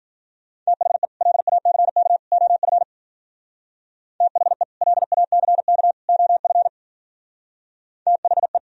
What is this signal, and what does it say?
Morse code, heard as a keyed tone, about 48 wpm, 700 Hz. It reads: THE LACK OF THE LACK OF THE
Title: Speed-Racing - Sets of 3 Words 40wpm